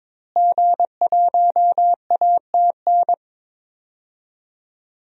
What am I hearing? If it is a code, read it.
Morse code, heard as a keyed tone, about 22 wpm, 700 Hz. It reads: G1ATN
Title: DXCC-100 Call Signs - 5 Characters - Repeated 3x - 22wpm